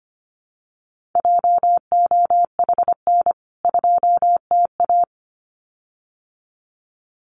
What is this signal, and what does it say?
Morse code, heard as a keyed tone, about 25 wpm, 700 Hz. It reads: JOHN 2TA
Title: CWT Exchange 25wpm